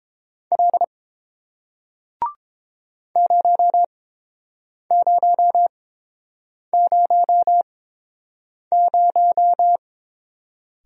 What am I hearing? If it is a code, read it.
Morse code, heard as a keyed tone, about 30 wpm, spaced as for 7 wpm, 700 Hz. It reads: L 0000